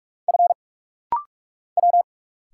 Morse code, heard as a keyed tone, about 46 wpm, 700 Hz. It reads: F W
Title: Single Letter-Number Alt - Rapid-Fire 45wpm